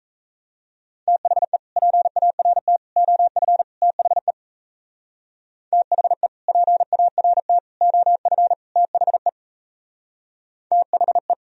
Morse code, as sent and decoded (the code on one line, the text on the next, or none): - .... / .--. .- .-. - / --- ..-. / - .... / - .... / .--. .- .-. - / --- ..-. / - .... / - .... .
THE PART OF THE THE PART OF THE THE